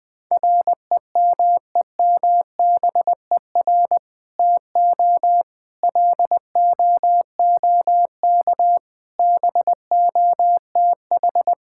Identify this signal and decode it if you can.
Morse code, heard as a keyed tone, about 20 wpm, 700 Hz. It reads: REMEMBER TO LOOK BOTH
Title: Sentences with 2-Word Phrases - 20wpm